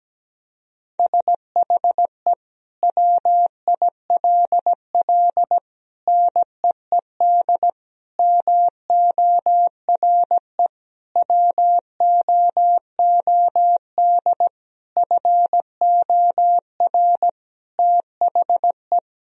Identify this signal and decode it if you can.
Morse code, heard as a keyed tone, about 17 wpm, 700 Hz. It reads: SHE WILL NEED MORE WOOD FOR THE
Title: Sentences from Top 1000 Words - Review Part 2 - 17wpm